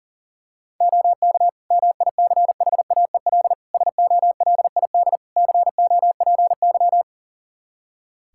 Morse code, heard as a keyed tone, about 40 wpm, 700 Hz. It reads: OK MICHAEL SOLID COPY
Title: Practice QSO - Exchange 3 - More Info - 40wpm